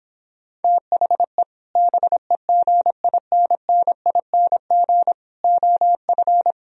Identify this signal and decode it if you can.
Morse code, heard as a keyed tone, about 26 wpm, 700 Hz. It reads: THE BEGINNING OF